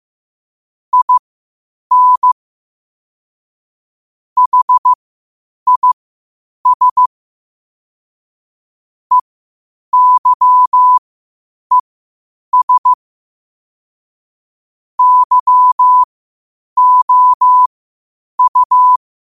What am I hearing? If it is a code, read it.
Morse code, heard as a keyed tone, about 15 wpm, spaced as for 8 wpm, 1000 Hz. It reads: IN HIS EYES YOU
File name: qotdfiles / 8wpm